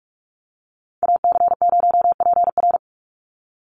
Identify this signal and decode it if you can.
Morse code, heard as a keyed tone, about 46 wpm, 700 Hz. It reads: AC0PR
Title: NA SOTA Activator Callsigns 45wpm